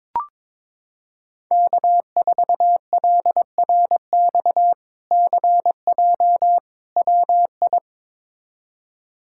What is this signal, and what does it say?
Morse code, heard as a keyed tone, about 22 wpm, 700 Hz. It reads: K4LRX CJ WI
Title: K1USN SST QSO Practice - 22wpm